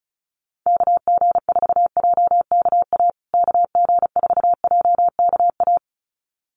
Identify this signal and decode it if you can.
Morse code, heard as a keyed tone, about 34 wpm, 700 Hz. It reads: KG4JKA KG4JKA
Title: USA Call Signs Repeated 2x 35wpm